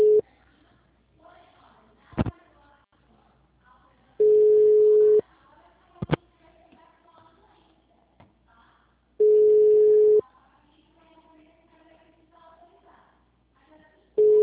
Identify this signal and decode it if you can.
Ringback tone (Europe, 425 Hz).